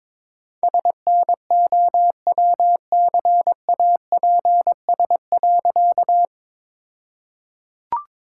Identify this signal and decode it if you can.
Morse code, heard as a keyed tone, about 22 wpm, 700 Hz. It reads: SNOWCAPS.
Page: morse-code-ninja-practice-22wpm